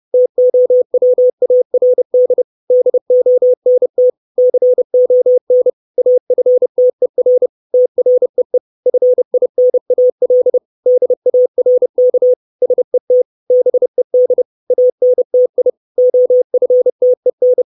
Morse code, heard as a keyed tone, about 30 wpm, 500 Hz. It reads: TOWARD DONT CON AFTER TREE FINAL DARK SET BED ANTI OFTEN
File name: 632 Words 30WPM